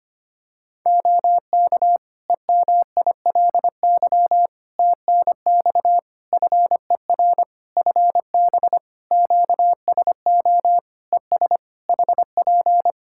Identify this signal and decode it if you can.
Morse code, heard as a keyed tone, about 25 wpm, 700 Hz. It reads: OK EMILY TNX FER FB QSO ES HP
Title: Practice QSO - Exchange 4 - Ending - 25wpm